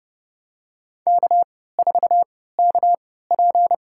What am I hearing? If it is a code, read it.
Morse code, heard as a keyed tone, about 30 wpm, spaced as for 17 wpm, 700 Hz. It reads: K4KP
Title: USA Call Signs - 4 Characters - 3x Character Spacing - 30wpm